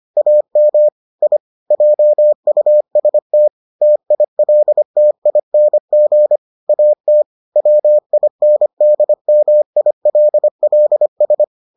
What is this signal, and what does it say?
Morse code, heard as a keyed tone, about 25 wpm, 600 Hz. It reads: AM I JUST TILTING AT WINDMILLS